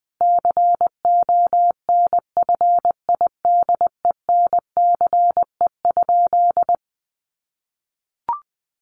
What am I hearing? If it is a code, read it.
Morse code, heard as a keyed tone, about 20 wpm, 700 Hz. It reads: CONFIDENCE?
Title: Sentences from Top 1500 Words 20wpm